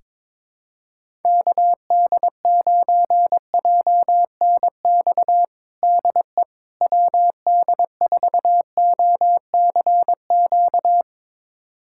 Morse code, heard as a keyed tone, about 21 wpm, 700 Hz. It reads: KD9JNX DE WD4OCQ
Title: USA Callsign DE Callsign 22wpm